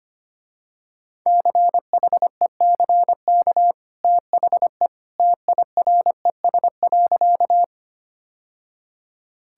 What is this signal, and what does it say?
Morse code, heard as a keyed tone, about 25 wpm, 700 Hz. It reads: CHECK THE TIRES.
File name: Easy Sentences with Hints - 25wpm